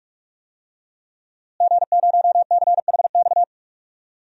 Morse code, heard as a keyed tone, about 42 wpm, 700 Hz. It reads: G0CHX
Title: DXCC-100 Call Signs - 5 Characters - Repeated 2x - 45wpm